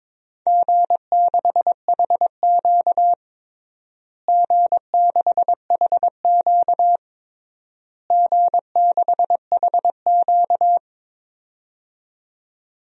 Morse code, heard as a keyed tone, about 22 wpm, 700 Hz. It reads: G6HQ G6HQ G6HQ
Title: DXCC-100 Call Signs - 4 Characters - Repeated 3x - 22wpm